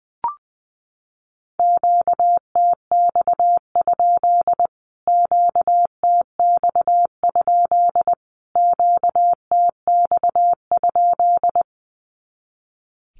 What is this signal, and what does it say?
Morse code, heard as a keyed tone, about 20 wpm, 700 Hz. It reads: QTX? QTX? QTX?
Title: Q-Codes Repeated 3x 20wpm